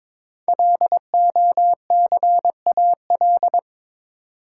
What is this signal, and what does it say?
Morse code, heard as a keyed tone, about 22 wpm, 700 Hz. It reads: LOCAL
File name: 5 Letter Words - First 3 Letters Given - 22wpm